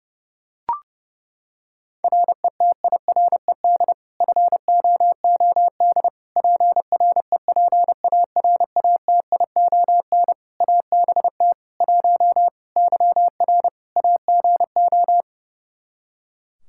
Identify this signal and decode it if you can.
Morse code, heard as a keyed tone, about 32 wpm, 700 Hz. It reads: RETIRED FOOD PREPARATION ABT 1 YR AGO
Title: Practice QSO - Exchange 3 - More Info - 30wpm